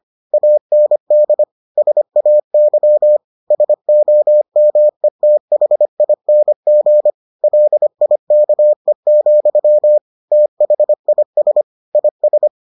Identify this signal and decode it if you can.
Morse code, heard as a keyed tone, about 25 wpm, 600 Hz. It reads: AND SAY SOMETHING LIKE, THIS IS